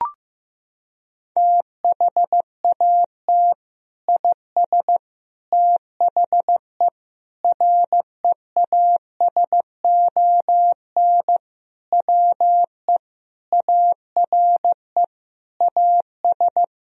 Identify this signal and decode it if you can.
Morse code, heard as a keyed tone, about 15 wpm, 700 Hz. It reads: THAT IS THE REASON WE ARE AS
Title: Sentences from Top 1000 Words - Review Part 4 - 15wpm